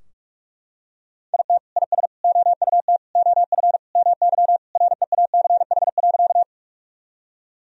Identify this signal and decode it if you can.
Morse code, heard as a keyed tone, about 41 wpm, 700 Hz. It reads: IT IS OUT OF MY REACH.
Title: Sentences from Top 1000 Words - Review Part 2 - 45wpm